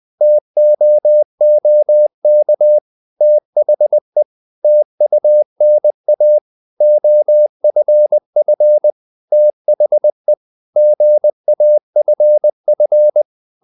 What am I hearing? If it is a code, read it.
Morse code, heard as a keyed tone, about 20 wpm, 600 Hz. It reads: TOOK THE TUNA OFF THE GAFF
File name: the old man and the sea / the old man and the sea part 1 20wpm